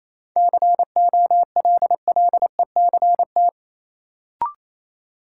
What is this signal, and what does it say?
Morse code, heard as a keyed tone, about 28 wpm, 700 Hz. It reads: COLLECT
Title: Sentences from Top 1600 Words - New Word Drill Alt - 28wpm